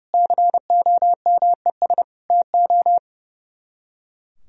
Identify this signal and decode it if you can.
Morse code, heard as a keyed tone, about 35 wpm, 700 Hz. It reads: COMES TO
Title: Speed-Racing - Sets of 5 Words 30wpm